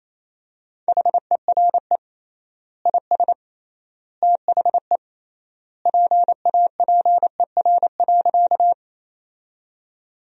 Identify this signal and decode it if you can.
Morse code, heard as a keyed tone, about 28 wpm, 700 Hz. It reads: HERE IS THE PAPER.
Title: 3x Word-Spacing - Sentences from Top 300 Words - 28wpm